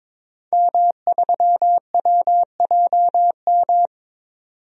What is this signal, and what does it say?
Morse code, heard as a keyed tone, about 23 wpm, 700 Hz. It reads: M3WJM
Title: DXCC-100 Call Signs - 5 Characters - 22wpm